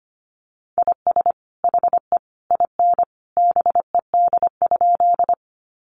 Morse code, heard as a keyed tone, about 25 wpm, 700 Hz. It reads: IS HE IN BED?
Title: Sentences from Top 500 Words 25wpm